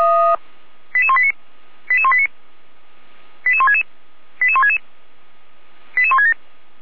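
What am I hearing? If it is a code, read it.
Five-tone selective call (ZVEI), heard as quick runs of five tones, tones of about 70 ms. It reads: 88189 88189 88166 88166 88167
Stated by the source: Five-tone ZVEI selective call, received on SDR.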